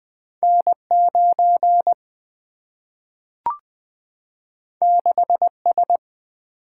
Morse code, heard as a keyed tone, about 20 wpm, 700 Hz. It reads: N 9 6 S
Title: Single-Letter Single-Number 20wpm